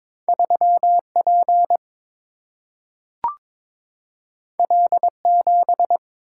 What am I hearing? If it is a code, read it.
Morse code, heard as a keyed tone, about 22 wpm, 700 Hz. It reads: 3 P L 7
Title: Single-Letter Single-Number 22wpm